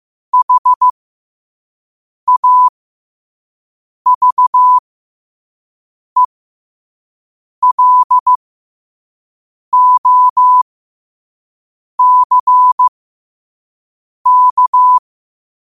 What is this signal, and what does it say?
Morse code, heard as a keyed tone, about 15 wpm, spaced as for 5 wpm, 1000 Hz. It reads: HAVELOCK
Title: 5 WPM morse code quotes for Fri, 01 Mar 2024